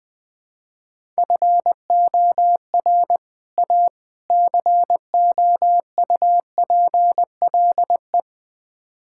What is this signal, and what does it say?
Morse code, heard as a keyed tone, about 20 wpm, 690 Hz. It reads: FOR A COUPLE
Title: Build A Phrase - 4-Word Phrases - 20wpm